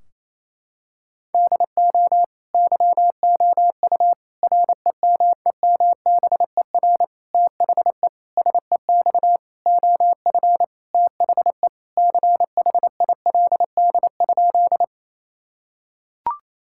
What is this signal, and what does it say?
Morse code, heard as a keyed tone, about 28 wpm, 700 Hz. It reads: DO YOU REMEMBER THE SEX OF THE CHILD?
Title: Sentences from Top 1600 Words 28wpm